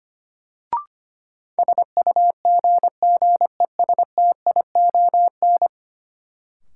Morse code, heard as a keyed tone, about 25 wpm, 700 Hz. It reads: SUGGESTION
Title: Sentences from Top 1600 Words - New Word Drill Alt - 25wpm